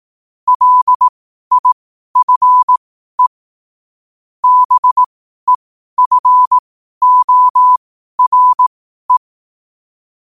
Morse code, heard as a keyed tone, about 18 wpm, spaced as for 12 wpm, 1000 Hz. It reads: LIFE BEFORE
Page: New quotes every day in morse code at 12 Words per minute.